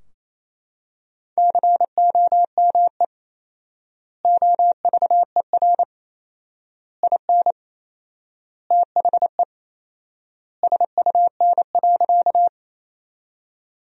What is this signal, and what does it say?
Morse code, heard as a keyed tone, about 28 wpm, 700 Hz. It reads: COME OVER IN THE SUN.
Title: 4x Word-Spacing - Sentences from Top 300 Words - 28wpm